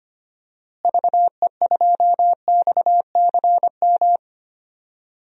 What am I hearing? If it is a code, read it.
Morse code, heard as a keyed tone, about 25 wpm, 700 Hz. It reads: VE2XCM
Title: DXCC-100 Call Signs - 6 Characters - Repeated 2x - 25wpm